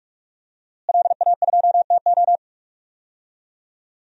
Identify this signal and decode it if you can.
Morse code, heard as a keyed tone, about 44 wpm, 700 Hz. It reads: LA2TO
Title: DXCC-100 Call Signs - 5 Characters - Repeated 2x - 45wpm